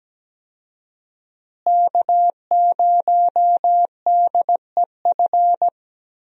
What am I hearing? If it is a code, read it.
Morse code, heard as a keyed tone, about 17 wpm, 700 Hz. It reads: K0DEF
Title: USA Call Signs - 5 Characters - Repeated 2x - 17wpm